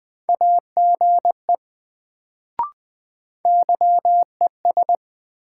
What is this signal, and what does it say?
Morse code, heard as a keyed tone, about 20 wpm, 700 Hz. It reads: AGE YES
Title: Sentences from Top 500 Words - New Word Drill - 20wpm